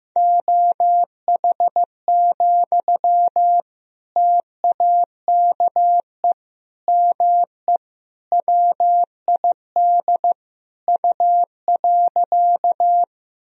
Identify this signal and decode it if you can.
Morse code, heard as a keyed tone, about 15 wpm, 700 Hz. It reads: OH, TAKE ME WID U.
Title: Sentences from Top 500 Words and Abbreviations 15wpm